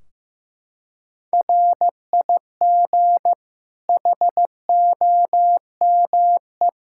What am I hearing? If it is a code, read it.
Morse code, heard as a keyed tone, about 15 wpm, 700 Hz. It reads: RIG HOME
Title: Practice QSO - Exchange 2 - Info - 15wpm